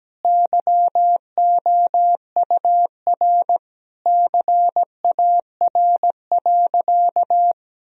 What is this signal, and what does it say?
Morse code, heard as a keyed tone, about 17 wpm, 700 Hz. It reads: YOUR CAR.
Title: Sentences from Top 1000 Words - Review Part 1 - 17wpm